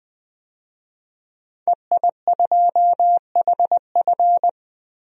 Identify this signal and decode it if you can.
Morse code, heard as a keyed tone, about 20 wpm, 700 Hz. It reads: EI2HF
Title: DXCC-100 Call Signs - 5 Characters - Repeated 2x - 20wpm